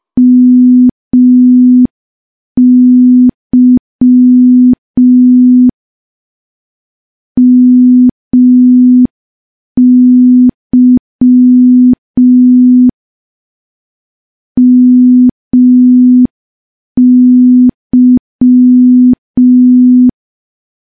Morse code, heard as a keyed tone, about 5 wpm, 250 Hz. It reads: MY MY MY
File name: morse / morse-training-05WPM